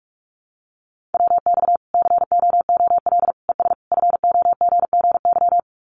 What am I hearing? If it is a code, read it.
Morse code, heard as a keyed tone, about 44 wpm, 700 Hz. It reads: WX COOL ES FOGGY
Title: Common QSO Phrases 45wpm